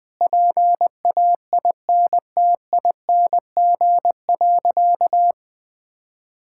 Morse code, heard as a keyed tone, about 20 wpm, 700 Hz. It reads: PAINTING.
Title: Sentences from Top 1000 Words - Review Part 6 - 20wpm